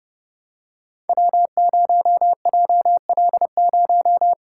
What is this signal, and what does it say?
Morse code, heard as a keyed tone, about 30 wpm, 700 Hz. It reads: W0JL0
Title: Code Groups - 5 Letter:Numbers - 30wpm